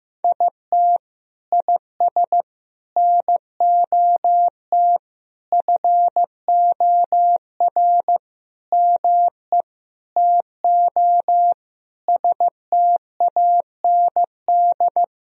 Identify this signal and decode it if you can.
Morse code, heard as a keyed tone, about 15 wpm, 700 Hz. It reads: IT IS NOT FOR ME TO STAND
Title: Sentences from Top 1000 Words - Review Part 1 - 15wpm